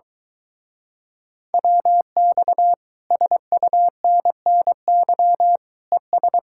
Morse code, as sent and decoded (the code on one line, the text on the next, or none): .-- -..- / ... ..- -. -. -.-- / . ...
WX SUNNY ES